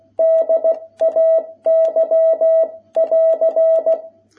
Morse code, heard as a keyed tone, about 16 wpm, 660 Hz. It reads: DAY<AR>
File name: jh8cbh / 2021cw